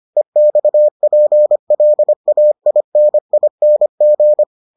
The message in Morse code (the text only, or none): EXPLAINING